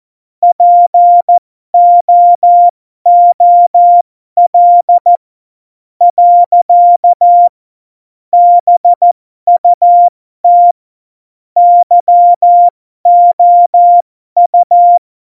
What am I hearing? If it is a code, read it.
Morse code, heard as a keyed tone, about 14 wpm, spaced as for 12 wpm, 700 Hz. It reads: POOL . BUT YOU
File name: Morsekurs / Witze